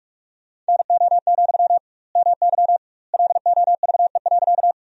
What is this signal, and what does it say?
Morse code, heard as a keyed tone, about 43 wpm, 700 Hz. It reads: NO, MY LOVE.
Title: 2x Word-Spacing - Sentences from Top 400 Words - 45wpm